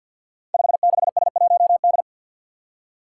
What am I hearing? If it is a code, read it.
Morse code, heard as a keyed tone, about 51 wpm, 700 Hz. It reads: 56S1D
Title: Five Challenging Letter-Number Combinations 50wpm